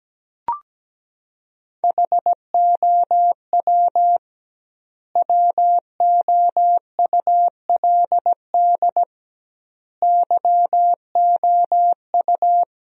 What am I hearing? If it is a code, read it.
Morse code, heard as a keyed tone, about 17 wpm, 700 Hz. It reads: HOW WOULD YOU
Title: 2x Word-Spacing - Sentences from Top 100 Words - 17wpm